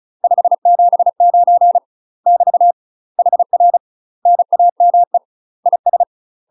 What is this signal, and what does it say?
Morse code, heard as a keyed tone, about 33 wpm, 700 Hz. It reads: 579 <BT> HR NAME IS
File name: UK MASTER QSOs TRAINING / 35WPM / UK MASTER QSOs 0036